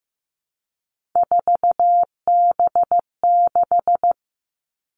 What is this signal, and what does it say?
Morse code, heard as a keyed tone, about 15 wpm, 700 Hz. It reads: 4B6
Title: Three Challenging Letter-Number Combinations 15wpm